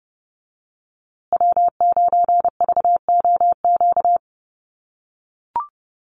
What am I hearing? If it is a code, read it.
Morse code, heard as a keyed tone, about 29 wpm, 700 Hz. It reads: W9VOQ
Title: USA Call Signs Repeated 3x 30wpm